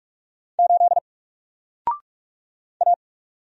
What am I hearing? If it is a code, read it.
Morse code, heard as a keyed tone, about 44 wpm, 700 Hz. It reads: 8 A
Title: Single Character (Letter:Number:Punctuation:Prosign) Alt 45wpm